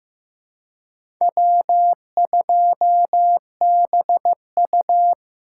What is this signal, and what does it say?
Morse code, heard as a keyed tone, about 15 wpm, 700 Hz. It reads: W2BU
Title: USA Callsign DE Callsign - 4 Characters - 15wpm